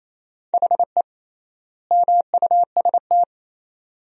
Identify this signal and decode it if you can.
Morse code, heard as a keyed tone, about 28 wpm, 700 Hz. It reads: HE MUST
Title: 3x Word-Spacing - Sentences from Top 500 Words - 28wpm